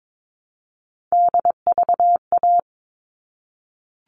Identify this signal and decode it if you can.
Morse code, heard as a keyed tone, about 22 wpm, 700 Hz. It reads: DVA
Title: All Cut Numbers - 3-Digit 22wpm